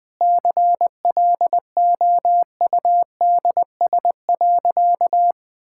Morse code, heard as a keyed tone, about 21 wpm, 700 Hz. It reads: CLOUDS.
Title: Easy Sentences with Hints - 20wpm